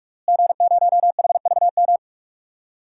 Morse code, heard as a keyed tone, about 46 wpm, 700 Hz. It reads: G0HVM
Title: DXCC-100 Call Signs - 5 Characters - Repeated 3x - 45wpm